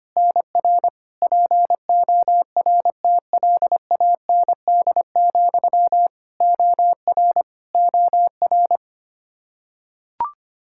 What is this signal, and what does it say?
Morse code, heard as a keyed tone, about 25 wpm, 700 Hz. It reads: NR PORTLAND, OR OR
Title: Practice QSO - Exchange 1 - Beginning - 25wpm